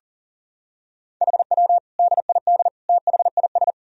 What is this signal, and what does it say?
Morse code, heard as a keyed tone, about 40 wpm, 700 Hz. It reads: HW DID THIS
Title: Sentences from Top 400 Words and Abbreviations 40wpm